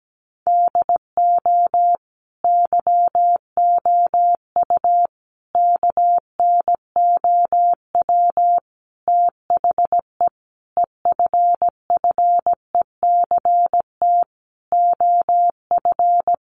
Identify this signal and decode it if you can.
Morse code, heard as a keyed tone, about 17 wpm, 700 Hz. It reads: DO YOU KNOW THE EFFECT OF
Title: Sentences from Top 1400 Words 17wpm